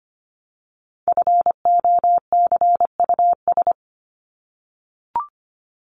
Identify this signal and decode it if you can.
Morse code, heard as a keyed tone, about 25 wpm, 700 Hz. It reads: FOCUS
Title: Top 501-750 Words 25wpm